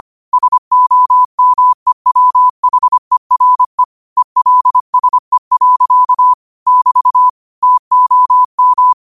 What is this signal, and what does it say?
Morse code, heard as a keyed tone, about 25 wpm, 1000 Hz. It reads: SOMEWHERE ELSE. <BT> TOM
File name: qotdfiles / 25wpm